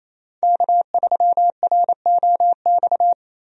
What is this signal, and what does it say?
Morse code, heard as a keyed tone, about 28 wpm, 700 Hz. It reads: K3ROX
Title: USA Call Signs - 5 Characters - 28wpm